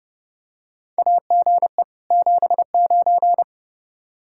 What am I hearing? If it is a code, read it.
Morse code, heard as a keyed tone, about 30 wpm, 700 Hz. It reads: AGE 79
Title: K1USN SST QSO Practice 30wpm